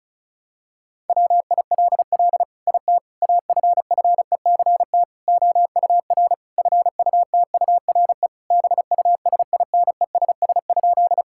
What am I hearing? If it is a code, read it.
Morse code, heard as a keyed tone, about 36 wpm, 700 Hz. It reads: WILL IT AFFECT OUR FUTURE BUSINESS?
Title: Sentences from Top 1000 Words - Review Part 6 - 35wpm